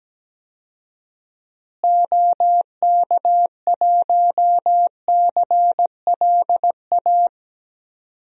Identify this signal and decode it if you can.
Morse code, heard as a keyed tone, about 17 wpm, 700 Hz. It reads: OK1CLA
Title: DXCC-100 Call Signs - 6 Characters - Repeated 3x - 17wpm